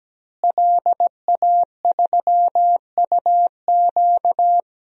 Morse code, heard as a keyed tone, about 17 wpm, 700 Hz. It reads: LA3UQ
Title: DXCC-100 Call Signs - 5 Characters - Repeated 2x - 17wpm